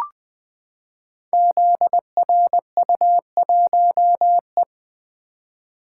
Morse code, heard as a keyed tone, about 20 wpm, 700 Hz. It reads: ZRU1E
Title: Code Groups - 5 Letter:Numbers - 20wpm